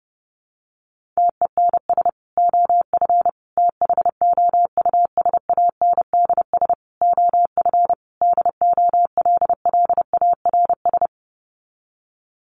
Morse code, heard as a keyed tone, about 30 wpm, 700 Hz. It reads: TENS OF THOUSANDS OF DOLLARS
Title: Sets of 5 Words Encore2 30wpm